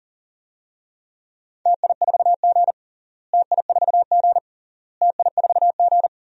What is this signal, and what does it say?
Morse code, heard as a keyed tone, about 40 wpm, 700 Hz. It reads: TI4G TI4G TI4G
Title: DXCC-100 Call Signs - 4 Characters - Repeated 3x - 40wpm